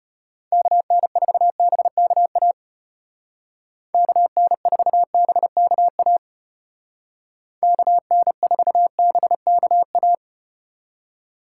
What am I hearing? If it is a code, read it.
Morse code, heard as a keyed tone, about 34 wpm, 700 Hz. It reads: KN4BKA KN4BKA KN4BKA